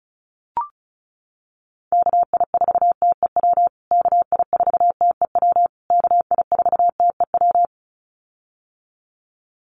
Morse code, heard as a keyed tone, about 34 wpm, 700 Hz. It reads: KI4TEW KI4TEW KI4TEW
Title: USA Call Signs Repeated 3x 35wpm